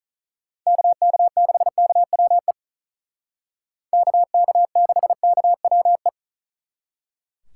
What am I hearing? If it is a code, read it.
Morse code, heard as a keyed tone, about 46 wpm, 700 Hz. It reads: KK6KWE KK6KWE